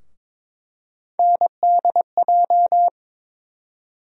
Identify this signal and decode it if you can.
Morse code, heard as a keyed tone, about 22 wpm, 700 Hz. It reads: NDJ
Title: USA Call Sign Postfixes - 3 Letters 22wpm